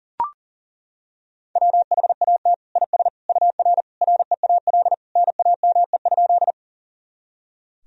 Morse code, heard as a keyed tone, about 40 wpm, 700 Hz. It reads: WHAT IS UR REAL NAME?
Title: Sentences from Top 300 Words and Abbreviations 40wpm